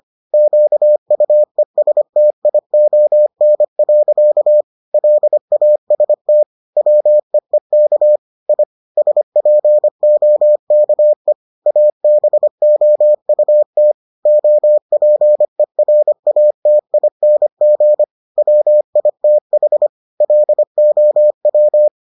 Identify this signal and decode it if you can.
Morse code, heard as a keyed tone, about 25 wpm, 600 Hz. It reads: QUESTION. LAST WEEK I SPOKE ABOUT OPERATING WITH LOW